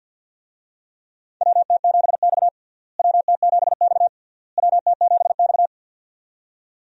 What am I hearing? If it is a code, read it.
Morse code, heard as a keyed tone, about 49 wpm, 700 Hz. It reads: WT7X WT7X WT7X
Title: USA Call Signs - 4 Characters - Repeated 3x - 50wpm